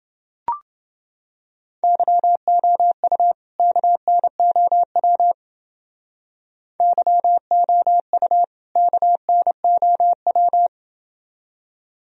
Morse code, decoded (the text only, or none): YOU KNOW YOU KNOW